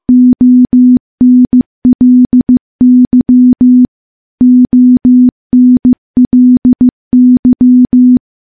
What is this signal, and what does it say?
Morse code, heard as a keyed tone, about 15 wpm, 250 Hz. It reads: ONLY ONLY
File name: morse-training-15WPM